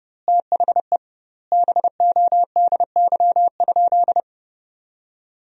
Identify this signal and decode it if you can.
Morse code, heard as a keyed tone, about 30 wpm, 700 Hz. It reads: THE BODY?
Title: 2x Word-Spacing - Sentences from Top 400 Words - 30wpm